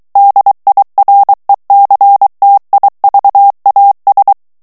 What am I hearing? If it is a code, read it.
Morse code, heard as a keyed tone, about 23 wpm, 800 Hz. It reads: DIRECTIVAS
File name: morse mp3 / ESPAGNOL TEXTE EN CLAIR 2 VITESSE 22 MOTS MINUTES